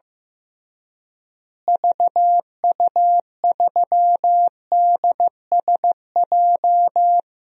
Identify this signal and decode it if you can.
Morse code, heard as a keyed tone, about 15 wpm, 700 Hz. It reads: VU3DSJ
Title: DXCC-100 Call Signs - 6 Characters - Repeated 3x - 15wpm